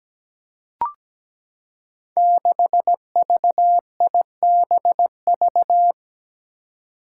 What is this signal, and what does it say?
Morse code, heard as a keyed tone, about 17 wpm, 700 Hz. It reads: 6VIBV
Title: Five Challenging Letter-Number Combinations 17wpm